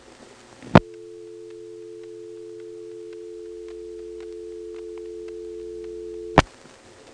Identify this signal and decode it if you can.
Dial tone (North America), 350 + 440 Hz.